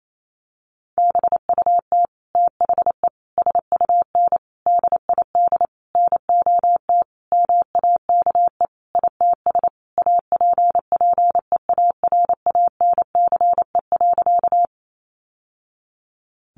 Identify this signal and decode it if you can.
Morse code, heard as a keyed tone, about 28 wpm, 700 Hz. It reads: BUT THE SUN DID NOT MAKE ITS APPEARANCE.
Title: Sentences from Top 1100 Words 28wpm